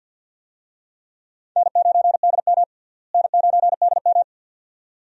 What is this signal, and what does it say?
Morse code, heard as a keyed tone, about 49 wpm, 700 Hz. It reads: N9DM N9DM
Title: USA Call Signs - 4 Characters - Repeated 2x - 50wpm